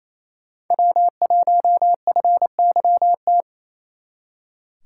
Morse code, heard as a keyed tone, about 35 wpm, 700 Hz. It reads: W1FYT